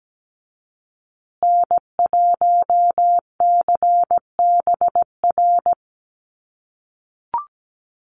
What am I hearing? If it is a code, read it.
Morse code, heard as a keyed tone, about 17 wpm, 700 Hz. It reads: N1CBR
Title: USA Call Signs Repeated 3x 17wpm